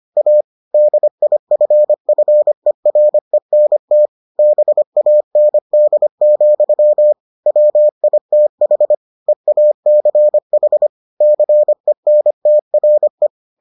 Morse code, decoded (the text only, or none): A DIFFERENT BAND, WITH EACH CENTRE